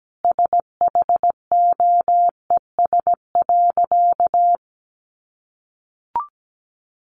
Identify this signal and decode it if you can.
Morse code, heard as a keyed tone, about 17 wpm, 700 Hz. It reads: SHOES.
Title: A Sound Of Thunder 17wpm